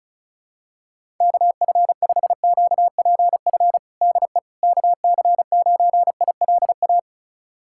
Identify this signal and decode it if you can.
Morse code, heard as a keyed tone, about 36 wpm, 700 Hz. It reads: KF5QPF DE KC9ILA
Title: USA Callsign DE Callsign 35wpm